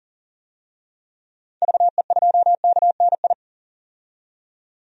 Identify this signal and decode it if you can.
Morse code, heard as a keyed tone, about 40 wpm, 700 Hz. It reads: VE2KNI
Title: DXCC-100 Call Signs - 6 Characters - 40wpm